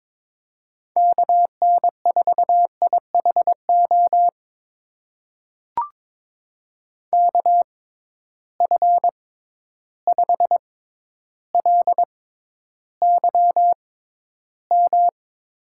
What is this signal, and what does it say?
Morse code, heard as a keyed tone, about 22 wpm, spaced as for 11 wpm, 700 Hz. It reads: KN4IHO KF5LYM
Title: USA Call Signs - 6 Characters - 6x Character Spacing - 22wpm